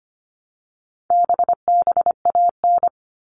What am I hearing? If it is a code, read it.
Morse code, heard as a keyed tone, about 25 wpm, 700 Hz. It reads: BBAN
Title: All Cut Numbers - 4-Digit 25wpm